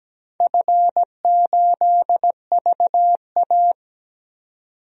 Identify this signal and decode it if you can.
Morse code, heard as a keyed tone, about 17 wpm, 700 Hz. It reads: F8VA
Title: DXCC-100 Call Signs - 4 Characters - 17wpm